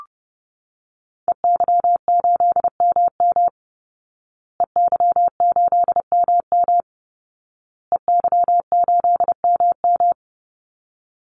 Morse code, heard as a keyed tone, about 30 wpm, 700 Hz. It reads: EY8MM EY8MM EY8MM
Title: Call Signs Repeated 2x - 30wpm